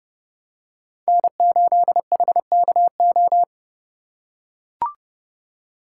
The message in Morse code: N8HKO

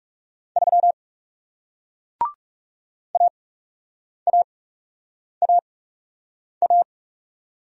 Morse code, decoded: 3 AAAA

